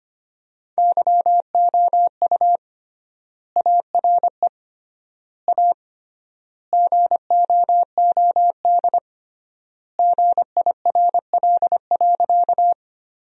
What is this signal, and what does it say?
Morse code, heard as a keyed tone, about 25 wpm, 700 Hz. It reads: YOU ARE A GOOD GIRL.
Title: 3x Word-Spacing - Sentences from Top 400 Words - 25wpm